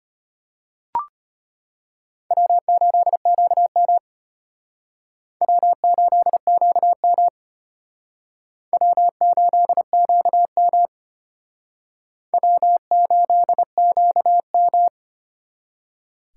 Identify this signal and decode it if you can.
Morse code, heard as a keyed tone, about 31 wpm, 700 Hz. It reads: W8QM W8QM W8QM W8QM